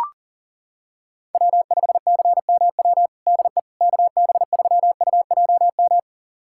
Morse code, heard as a keyed tone, about 37 wpm, 700 Hz. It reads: W5CMW DE KB3UJM
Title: USA Callsign DE Callsign 40wpm